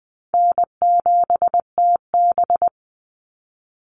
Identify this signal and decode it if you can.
Morse code, heard as a keyed tone, about 20 wpm, 700 Hz. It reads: N7TB
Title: Call Signs - 20wpm